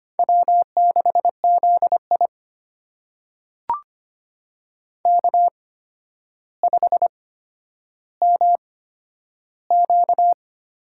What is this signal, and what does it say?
Morse code, heard as a keyed tone, about 25 wpm, spaced as for 11 wpm, 700 Hz. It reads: W6ZI K5MQ
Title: USA Call Signs - 4 Characters - 8x Character Spacing - 25wpm